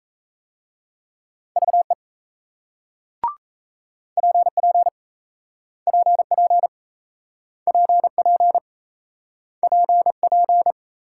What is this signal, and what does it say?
Morse code, heard as a keyed tone, about 38 wpm, 700 Hz. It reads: VE PP PP PP PP